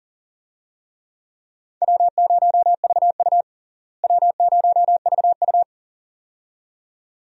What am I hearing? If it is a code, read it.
Morse code, heard as a keyed tone, about 40 wpm, 700 Hz. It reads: W0VU W0VU
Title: USA Call Signs - 4 Characters - Repeated 2x - 40wpm